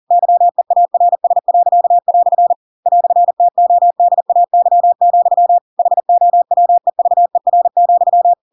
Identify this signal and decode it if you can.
Morse code, heard as a keyed tone, about 40 wpm, 700 Hz. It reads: YEARS." "TODAY, HOWEVER,
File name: Pinocchio / Pinocchio ch4 40 40 ik1hge